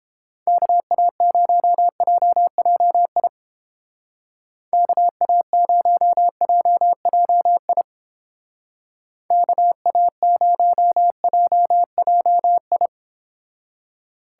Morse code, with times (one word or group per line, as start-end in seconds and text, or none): 0.5-3.3 KA0JJI
4.7-7.8 KA0JJI
9.3-12.9 KA0JJI